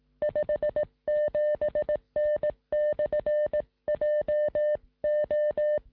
Morse code, heard as a keyed tone, about 18 wpm, 600 Hz. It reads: 57N/JO
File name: sm7lcb / 1958z ok1ri 2m